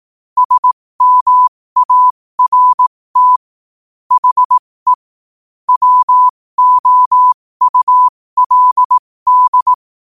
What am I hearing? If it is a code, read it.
Morse code, heard as a keyed tone, about 18 wpm, spaced as for 15 wpm, 1000 Hz. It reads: SMART HE WOULD
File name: qotdfiles / 15wpm